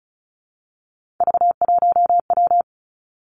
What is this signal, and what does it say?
Morse code, heard as a keyed tone, about 36 wpm, 700 Hz. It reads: V1W
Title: Three Challenging Letter-Number Combinations 35wpm